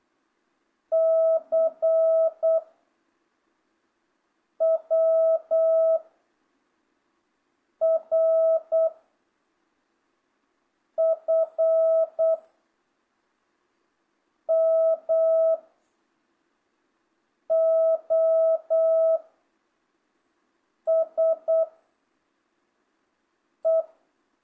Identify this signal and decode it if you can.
Morse code, heard as a keyed tone, about 8 wpm, 660 Hz. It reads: C W R F M O S E
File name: jh8cbh / 2025cw